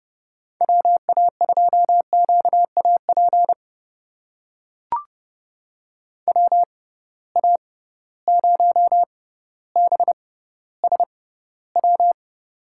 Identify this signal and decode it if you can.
Morse code, heard as a keyed tone, about 30 wpm, spaced as for 15 wpm, 700 Hz. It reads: WA2QAP WA0BSW
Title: USA Call Signs - 6 Characters - 6x Character Spacing - 30wpm